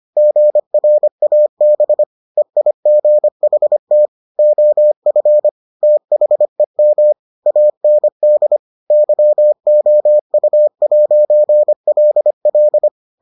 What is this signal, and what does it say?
Morse code, heard as a keyed tone, about 25 wpm, 600 Hz. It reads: GRAB EIGHT OF THEM AND YOU'LL